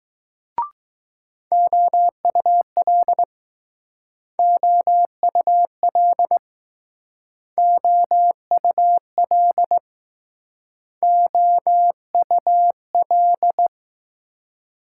Morse code, heard as a keyed tone, about 21 wpm, 700 Hz. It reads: OUL OUL OUL OUL